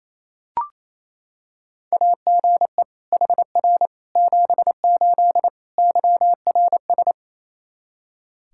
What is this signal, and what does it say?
Morse code, heard as a keyed tone, about 28 wpm, 700 Hz. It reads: AGE HR 78 YRS
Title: K1USN SST QSO Practice 28wpm